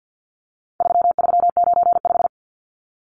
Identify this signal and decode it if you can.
Morse code, heard as a keyed tone, about 50 wpm, 700 Hz. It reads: 3395
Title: Numbers - 4-Digit 50wpm